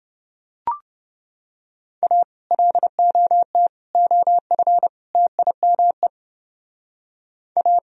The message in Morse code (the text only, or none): A LOT OF TIME A